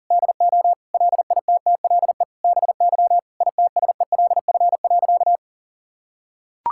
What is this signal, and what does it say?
Morse code, heard as a keyed tone, about 40 wpm, 700 Hz. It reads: DO LITTLE BY ITSELF.
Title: Sentences from Top 1000 Words - Review Part 5 - 40wpm